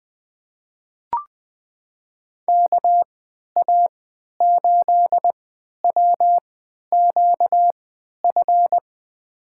Morse code, heard as a keyed tone, about 21 wpm, spaced as for 12 wpm, 700 Hz. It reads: KA8WQF
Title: USA Call Signs - 6 Characters - 3x Character Spacing - 20wpm